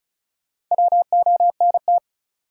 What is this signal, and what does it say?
Morse code, heard as a keyed tone, about 36 wpm, 700 Hz. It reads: WONT
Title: Sentences from Top 1000 Words - Review Part 6 - New Word Drill - 35wpm